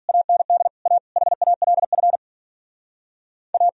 Morse code, heard as a keyed tone, about 52 wpm, 700 Hz. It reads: AND A HALF A